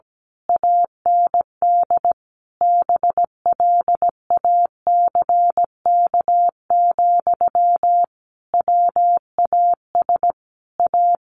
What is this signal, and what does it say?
Morse code, heard as a keyed tone, about 17 wpm, 700 Hz. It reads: AND BLACK, WAS A
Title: A Sound Of Thunder 17wpm